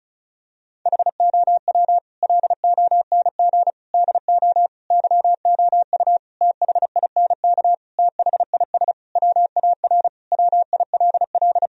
Morse code, heard as a keyed tone, about 34 wpm, 700 Hz. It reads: HOW LONG DO YOU THINK THIS WAR WILL
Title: Sentences from Top 1000 Words - Review Part 2 - 35wpm